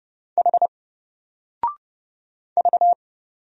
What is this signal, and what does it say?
Morse code, heard as a keyed tone, about 29 wpm, 700 Hz. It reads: H V
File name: Single-Letter-Number-30wpm